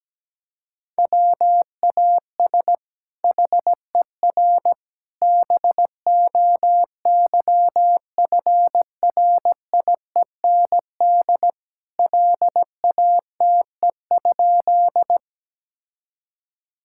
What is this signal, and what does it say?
Morse code, heard as a keyed tone, about 17 wpm, 700 Hz. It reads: WAS HER BOYFRIEND LATE?
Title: Sentences from Top 1000 Words - Review Part 5 - 17wpm